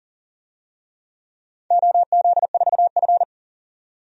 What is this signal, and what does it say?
Morse code, heard as a keyed tone, about 40 wpm, 700 Hz. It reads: OZ4F
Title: DXCC-100 Call Signs - 4 Characters - 40wpm